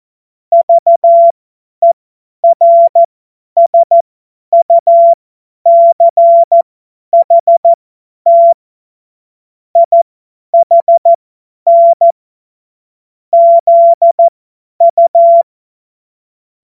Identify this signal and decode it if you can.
Morse code, heard as a keyed tone, about 14 wpm, spaced as for 10 wpm, 680 Hz. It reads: VERSUCHT IHN ZU
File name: Morsekurs / Witze / Witz 06. Dezember 2022 50 F70